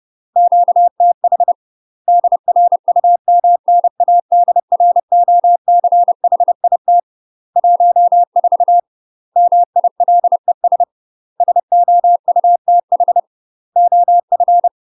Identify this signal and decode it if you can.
Morse code, heard as a keyed tone, about 30 wpm, 700 Hz. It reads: QTH DRUMNADROCHIT 14 MILES SOUTH OF
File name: UK MASTER QSOs TRAINING / 30WPM / UK MASTER QSOs 0034